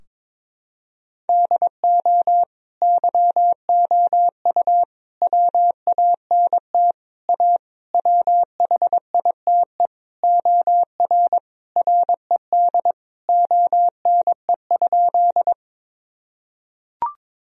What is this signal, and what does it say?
Morse code, heard as a keyed tone, about 22 wpm, 700 Hz. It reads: DO YOU WANT A WHITE OR RED ONE?
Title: Sentences from Top 1000 Words - Review Part 2 - 22wpm